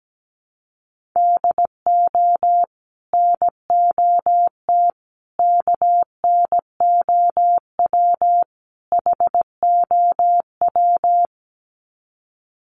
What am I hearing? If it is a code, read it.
Morse code, heard as a keyed tone, about 17 wpm, 700 Hz. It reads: DO NOT KNOW HOW
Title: Sets of 4 Words 17wpm